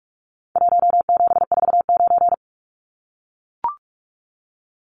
Morse code, heard as a keyed tone, about 44 wpm, 700 Hz. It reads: 1749
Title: Numbers - Contest Style - 4-Digit 45wpm